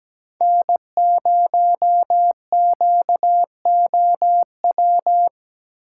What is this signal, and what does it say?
Morse code, heard as a keyed tone, about 17 wpm, 700 Hz. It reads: N0QOW
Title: USA Call Signs - 6x Character Spacing - 17wpm